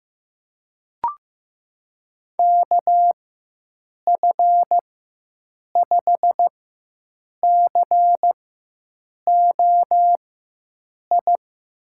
Morse code, heard as a keyed tone, about 15 wpm, spaced as for 7 wpm, 700 Hz. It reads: KF5COI
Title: USA Call Signs - 6 Characters - 4x Character Spacing - 15wpm